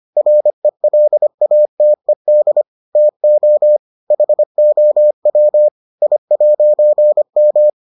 Morse code, heard as a keyed tone, about 25 wpm, 600 Hz. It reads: RELATED TO HOW I'M